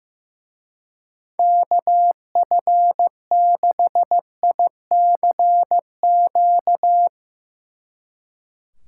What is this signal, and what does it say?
Morse code, heard as a keyed tone, about 16 wpm, 700 Hz. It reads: KF6ICQ
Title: Speed Racing - USA Call Signs 15wpm